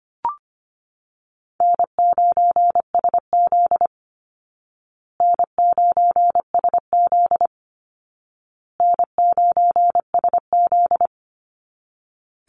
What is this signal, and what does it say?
Morse code, heard as a keyed tone, about 25 wpm, 700 Hz. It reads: N9SZ N9SZ N9SZ
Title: Easy Call Signs Repeated 3x - 25wpm